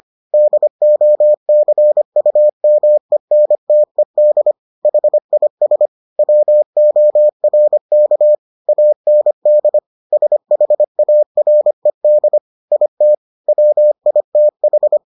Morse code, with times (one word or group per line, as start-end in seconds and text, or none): 0.3-4.5 DOCUMENTED
4.8-5.9 HIS
6.2-8.3 WORK
8.7-9.8 AND
10.1-12.4 SHARED
12.7-13.1 IT
13.5-15.0 WITH